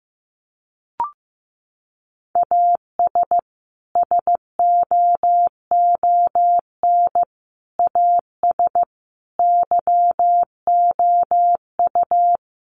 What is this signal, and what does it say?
Morse code, heard as a keyed tone, about 15 wpm, 700 Hz. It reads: AS SOON AS YOU
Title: Sentences from Top 1500 Words 15wpm